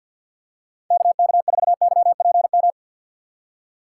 Morse code, heard as a keyed tone, about 51 wpm, 690 Hz. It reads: KK4YPM
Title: USA Call Signs - 6 Characters - 3x Character Spacing - 50wpm